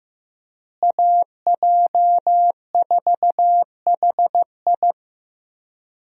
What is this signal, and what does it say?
Morse code, heard as a keyed tone, about 15 wpm, 700 Hz. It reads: AJ4HI
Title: USA Call Signs - 5 Characters - Repeated 3x - 15wpm